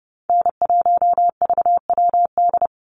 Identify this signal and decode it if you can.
Morse code, heard as a keyed tone, about 30 wpm, 700 Hz. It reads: N1VWD
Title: NA SOTA Chaser Callsigns 30wpm